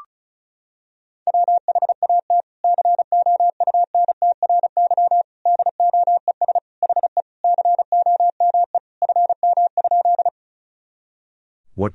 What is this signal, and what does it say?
Morse code, heard as a keyed tone, about 34 wpm, 700 Hz. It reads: WHAT COUNTRY DOES HE COME FM?
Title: Sentences from Top 300 Words and Abbreviations 35wpm